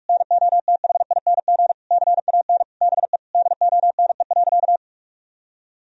Morse code, heard as a keyed tone, about 41 wpm, 700 Hz. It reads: NOTHING CAN BE DONE.
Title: Sentences from Top 1000 Words - Review Part 1 - 45wpm